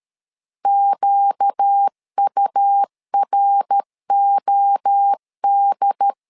Morse code, heard as a keyed tone, about 13 wpm, 800 Hz. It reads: QUROD
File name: LEZIONI / 4 Q 12wpm 1833